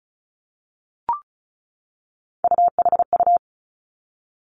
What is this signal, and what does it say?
Morse code, heard as a keyed tone, about 36 wpm, 700 Hz. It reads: UHU
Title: Three Challenging Letter-Number Combinations 35wpm